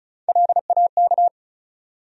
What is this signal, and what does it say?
Morse code, heard as a keyed tone, about 34 wpm, 700 Hz. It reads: LAK
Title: USA Call Sign Postfixes - 3 Letters 35wpm